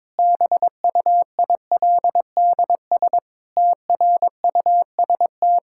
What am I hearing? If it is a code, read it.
Morse code, heard as a keyed tone, about 22 wpm, 700 Hz. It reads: BUILDS TRUST
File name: Difficult Sentences with Hints - 22wpm